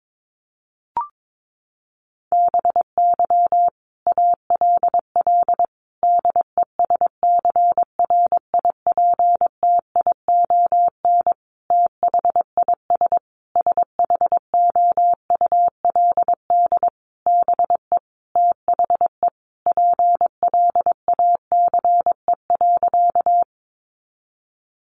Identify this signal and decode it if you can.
Morse code, heard as a keyed tone, about 22 wpm, 700 Hz. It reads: BY ALL DESCRIPTION THIS SHOULD BE THE PLACE.
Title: Sentences from Top 1300 Words 22wpm